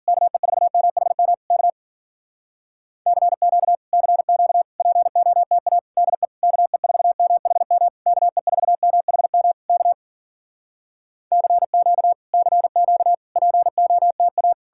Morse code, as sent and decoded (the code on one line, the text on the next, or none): -.- . ....- -- .... -- / -.- / -.-. --.- / -.-. --.- / .--. --- - .- / -.. / -.- . ....- -- .... -- / -.- . ....- -- .... -- / -.- / -.-. --.- / -.-. --.- / .--. --- - .-
KE4MHM K CQ CQ POTA DE KE4MHM KE4MHM K CQ CQ POTA